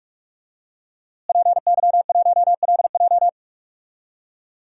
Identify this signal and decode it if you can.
Morse code, heard as a keyed tone, about 44 wpm, 700 Hz. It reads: PY1LJ
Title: DXCC-100 Call Signs - 5 Characters - Repeated 2x - 45wpm